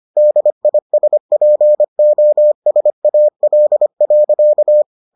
Morse code, heard as a keyed tone, about 25 wpm, 600 Hz. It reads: DISPOSAL.